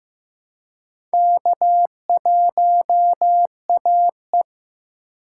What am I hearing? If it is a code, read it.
Morse code, heard as a keyed tone, about 15 wpm, 700 Hz. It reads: K1AE
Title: K1USN SST QSO Practice - 15wpm